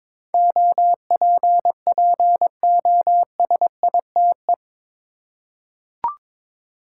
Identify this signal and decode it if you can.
Morse code, heard as a keyed tone, about 21 wpm, 700 Hz. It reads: OPPOSITE
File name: practice / 22wpm / Eight Letter Words - 22wpm